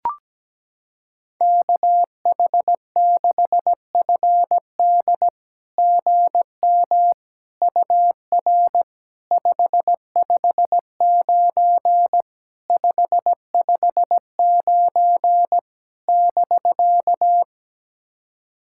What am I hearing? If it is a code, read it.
Morse code, heard as a keyed tone, about 17 wpm, 690 Hz. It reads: KH6FD GM UR 559 559 <BK>
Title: SOTA QSO Practice 17wpm